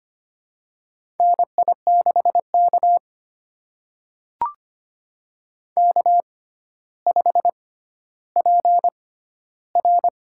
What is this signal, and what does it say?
Morse code, heard as a keyed tone, about 25 wpm, spaced as for 13 wpm, 700 Hz. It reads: NI6K K5PR
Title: USA Call Signs - 4 Characters - 6x Character Spacing - 25wpm